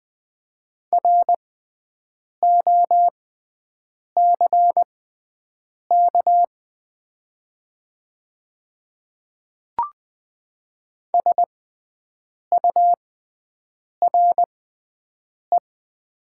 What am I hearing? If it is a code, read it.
Morse code, heard as a keyed tone, about 20 wpm, spaced as for 7 wpm, 700 Hz. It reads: ROCK SURE
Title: 6x Character Spacing - Top 500 Words - 20wpm